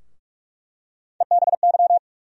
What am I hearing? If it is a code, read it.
Morse code, heard as a keyed tone, about 46 wpm, 700 Hz. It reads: EBY
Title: USA Call Sign Postfixes - 3 Letters 45wpm